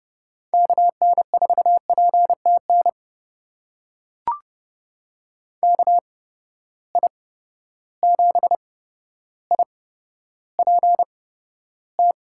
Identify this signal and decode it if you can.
Morse code, heard as a keyed tone, about 31 wpm, spaced as for 13 wpm, 700 Hz. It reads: KN4PTN KI7IPT
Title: USA Call Signs - 6 Characters - 8x Character Spacing - 30wpm